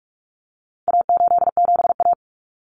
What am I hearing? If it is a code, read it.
Morse code, heard as a keyed tone, about 44 wpm, 700 Hz. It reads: A87A
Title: Common Cut Numbers - 4-Digit 45wpm